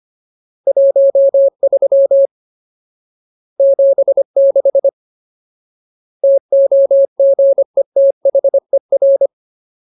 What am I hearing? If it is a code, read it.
Morse code, heard as a keyed tone, about 25 wpm, 550 Hz. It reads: 13 76 TOGETHER
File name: Ultimate Word List / 25WPM Ultimate